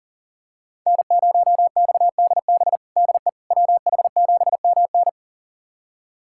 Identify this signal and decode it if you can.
Morse code, heard as a keyed tone, about 40 wpm, 690 Hz. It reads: N0XDB DE WH7MN
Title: USA Callsign DE Callsign - 5 Characters - 40wpm